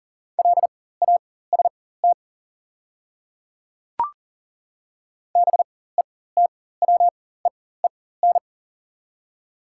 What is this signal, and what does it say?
Morse code, heard as a keyed tone, about 40 wpm, spaced as for 19 wpm, 700 Hz. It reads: LAST BETWEEN
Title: 4x Character Spacing - Top 300 Words - 40wpm